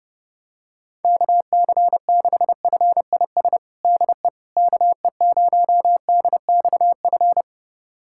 Morse code, heard as a keyed tone, about 30 wpm, 700 Hz. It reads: KC6FIS DE KE0DXF
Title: USA Callsign DE Callsign - 6 Characters - 30wpm